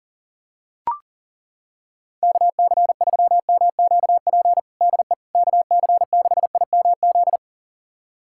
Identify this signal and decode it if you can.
Morse code, heard as a keyed tone, about 40 wpm, 700 Hz. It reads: KC3MQP DE KC6IMZ
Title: USA Callsign DE Callsign 40wpm